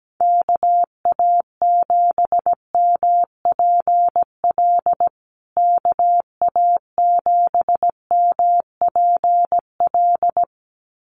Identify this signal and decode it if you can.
Morse code, heard as a keyed tone, about 17 wpm, 700 Hz. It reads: KA7MPL KA7MPL
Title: USA Call Signs Repeated 2x 17wpm